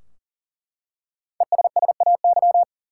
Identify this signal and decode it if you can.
Morse code, heard as a keyed tone, about 40 wpm, 700 Hz. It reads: ESSAY
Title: Five Letter Words - 40wpm